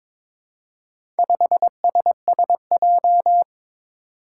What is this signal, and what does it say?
Morse code, heard as a keyed tone, about 22 wpm, 700 Hz. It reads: 5SSJ
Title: Four Challenging Letter-Number Combinations 22wpm